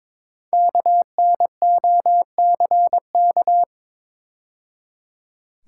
Morse code, heard as a keyed tone, about 22 wpm, 700 Hz. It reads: KNOCK
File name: 5 Letter Words - First 3 Letters Given - 22wpm